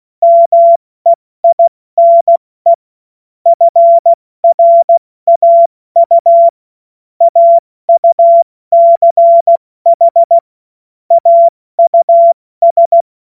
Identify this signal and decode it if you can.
Morse code, heard as a keyed tone, about 16 wpm, 680 Hz. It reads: MEINE FRAU AUCH AUS
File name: Witz-22-Juli-2025-70-F80